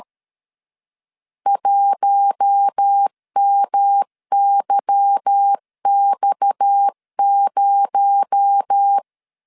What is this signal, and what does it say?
Morse code, heard as a keyed tone, about 13 wpm, 800 Hz. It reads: 1MYX0
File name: LEZIONI / 10 SIGNS / 10A SLASH 12wpm 2314